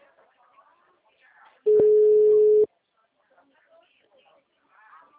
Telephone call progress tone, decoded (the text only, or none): ringback tone